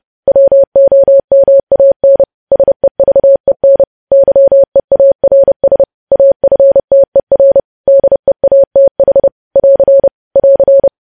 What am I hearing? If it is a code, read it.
Morse code, heard as a keyed tone, about 30 wpm, 550 Hz. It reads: WOMAN SEVEN YEARS AFTER DEATH <AR> <AR>
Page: news headlines in morse code at 30 wpm